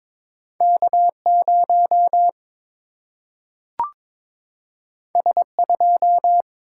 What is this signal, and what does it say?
Morse code, heard as a keyed tone, about 22 wpm, 700 Hz. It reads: K 0 S 2
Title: Single-Letter Single-Number 22wpm